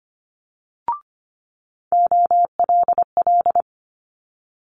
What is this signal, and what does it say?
Morse code, heard as a keyed tone, about 25 wpm, 700 Hz. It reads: OLL
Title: Multi-letter Phonemes Quick Fire 25wpm